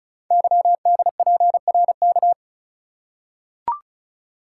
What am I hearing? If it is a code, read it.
Morse code, heard as a keyed tone, about 36 wpm, 690 Hz. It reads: YDPRK
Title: Code Groups - 5 Letters - 35wpm